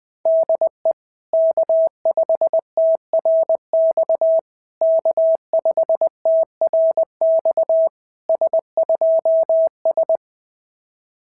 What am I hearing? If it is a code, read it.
Morse code, heard as a keyed tone, about 18 wpm, 650 Hz. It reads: DE K5TRX K5TRX S2S
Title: SOTA QSO Practice 20wpm